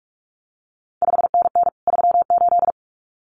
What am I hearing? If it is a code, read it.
Morse code, heard as a keyed tone, about 46 wpm, 700 Hz. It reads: 5NN 38
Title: CQ WW DX Contest Exchange Encore - 45wpm